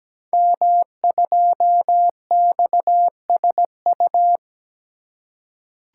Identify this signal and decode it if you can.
Morse code, heard as a keyed tone, about 17 wpm, 700 Hz. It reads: M2XSU
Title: Code Groups - 5 Letter:Numbers - 17wpm